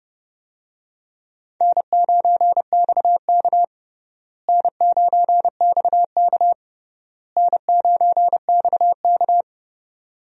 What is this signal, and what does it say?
Morse code, heard as a keyed tone, about 31 wpm, 700 Hz. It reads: N9XK N9XK N9XK
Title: USA Call Signs - 4 Characters - Repeated 3x - 30wpm